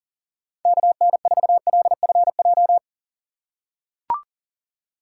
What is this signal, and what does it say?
Morse code, heard as a keyed tone, about 40 wpm, 700 Hz. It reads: KN4LFJ
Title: USA Call Signs - 2x Character Spacing - 40wpm